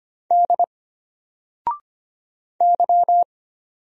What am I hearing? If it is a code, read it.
Morse code, heard as a keyed tone, about 25 wpm, 700 Hz. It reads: D Y
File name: Single-Letter-Number-25wpm